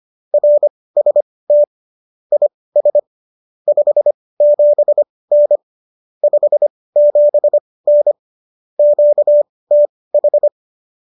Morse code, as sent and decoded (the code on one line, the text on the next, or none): .-. ... - / .. ... / ..... --... -. / ..... --... -. / --.- - ....
RST IS 57N 57N QTH